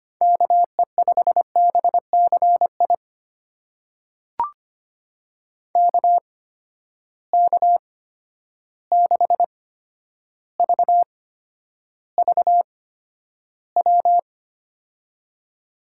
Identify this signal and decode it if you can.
Morse code, heard as a keyed tone, about 25 wpm, spaced as for 11 wpm, 700 Hz. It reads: KE5BCI KK6VVW
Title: USA Call Signs - 6 Characters - 8x Character Spacing - 25wpm